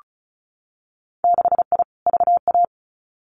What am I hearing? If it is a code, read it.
Morse code, heard as a keyed tone, about 34 wpm, 700 Hz. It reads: 6I VA
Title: Winter Field Day Exchanges Encore 35wpm